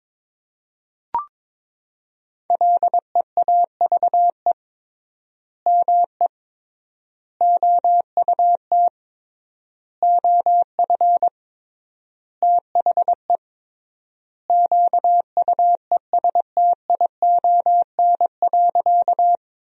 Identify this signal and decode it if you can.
Morse code, heard as a keyed tone, about 22 wpm, 700 Hz. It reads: LEAVE ME OUT OF THE QUESTION.
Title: 3x Word-Spacing - Sentences from Top 400 Words - 22wpm